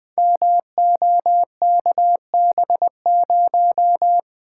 Morse code, heard as a keyed tone, about 20 wpm, 700 Hz. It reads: MOKB0
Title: Code Groups - 5 Letter:Numbers - 20wpm